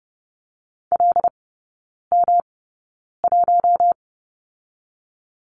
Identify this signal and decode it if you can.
Morse code, heard as a keyed tone, about 30 wpm, spaced as for 9 wpm, 700 Hz. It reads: LM1
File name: Lesson 18e - Letter m - Three Characters 30wpm